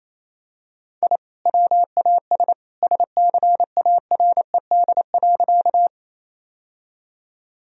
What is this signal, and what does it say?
Morse code, heard as a keyed tone, about 28 wpm, 700 Hz. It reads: I WAS SCARED.
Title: Sentences from Top 1000 Words - Review Part 4 - 28wpm